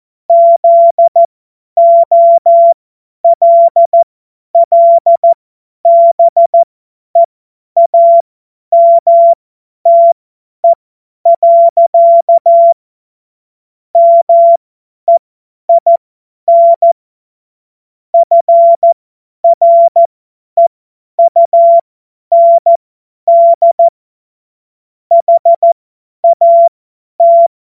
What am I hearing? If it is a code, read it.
Morse code, heard as a keyed tone, about 14 wpm, spaced as for 10 wpm, 680 Hz. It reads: ZOLLBEAMTE. MEIN FREUND HAT
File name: Morsekurs / Witze / Witz 28. März 2023 50 F70